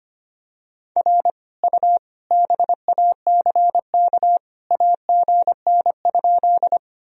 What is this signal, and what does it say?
Morse code, heard as a keyed tone, about 25 wpm, 700 Hz. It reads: R U BACK AGN?
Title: Sentences from Top 200 Words and Abbreviations 25wpm